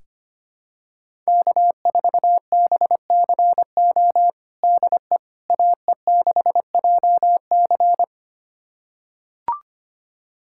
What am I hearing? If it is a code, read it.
Morse code, heard as a keyed tone, about 25 wpm, 700 Hz. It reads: K4BCO DE AE6JC
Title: USA Callsign DE Callsign - 5 Characters - 25wpm